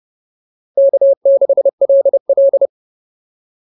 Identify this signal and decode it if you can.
Morse code, heard as a keyed tone, about 30 wpm, 560 Hz. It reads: K6LL